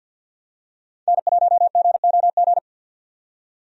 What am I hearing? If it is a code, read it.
Morse code, heard as a keyed tone, about 49 wpm, 700 Hz. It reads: N1GOG
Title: USA Call Signs - 5 Characters - 50wpm